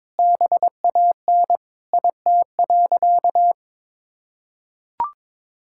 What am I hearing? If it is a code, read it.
Morse code, heard as a keyed tone, about 22 wpm, 700 Hz. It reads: BAN IT.
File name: practice / 22wpm / Sentences with 8-Word Phrases - 22wpm